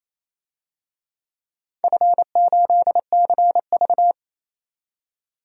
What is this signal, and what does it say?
Morse code, heard as a keyed tone, about 28 wpm, 700 Hz. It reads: F8CV
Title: DXCC-100 Call Signs - 4 Characters - 28wpm